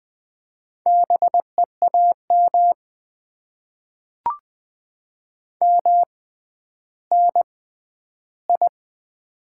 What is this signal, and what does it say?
Morse code, heard as a keyed tone, about 20 wpm, spaced as for 11 wpm, 700 Hz. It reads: BEAM MNI
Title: Top 90 QSO Elements - 6x Character Spacing - 20wpm